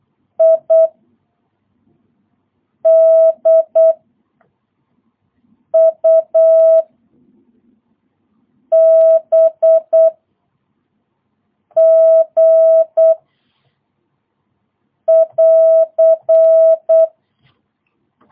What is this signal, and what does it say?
Morse code, heard as a keyed tone, about 8 wpm, 650 Hz. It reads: I D U B G <AR>
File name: jh8cbh / 2022cw